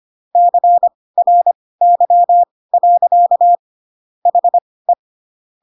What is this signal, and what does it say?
Morse code, heard as a keyed tone, about 25 wpm, spaced as for 18 wpm, 700 Hz. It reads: CRY. HE
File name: Pinocchio / Pinocchio ch1 12 25 ik1hge